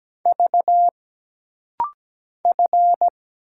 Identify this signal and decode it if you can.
Morse code, heard as a keyed tone, about 17 wpm, 700 Hz. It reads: V F
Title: Single Letters - Rapid-Fire 17wpm